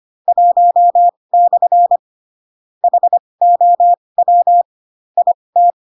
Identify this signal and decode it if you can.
Morse code, heard as a keyed tone, about 25 wpm, spaced as for 20 wpm, 700 Hz. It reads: 1/ HOW IT
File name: Pinocchio / Pinocchio ch1 15 25 ik1hge